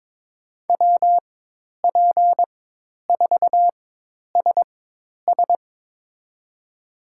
Morse code, heard as a keyed tone, about 22 wpm, spaced as for 10 wpm, 700 Hz. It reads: WP4SS
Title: USA Call Signs - 5 Characters - 4x Character Spacing - 22wpm